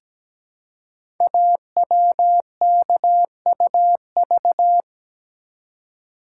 Morse code, heard as a keyed tone, about 17 wpm, 700 Hz. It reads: AWKUV
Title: Code Groups - 5 Letters - 17wpm